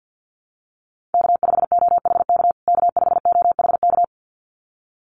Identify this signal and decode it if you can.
Morse code, heard as a keyed tone, about 50 wpm, 700 Hz. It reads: K5OHK K5OHK
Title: USA Call Signs Repeated 2x 50wpm